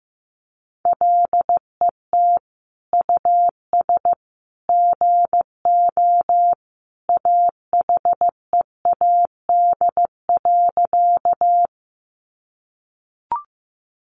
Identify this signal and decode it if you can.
Morse code, heard as a keyed tone, about 15 wpm, 700 Hz. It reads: LET US GO AHEAD.
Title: Sentences from Top 1100 Words 15wpm